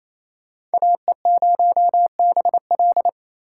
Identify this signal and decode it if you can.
Morse code, heard as a keyed tone, about 37 wpm, 700 Hz. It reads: AE0BL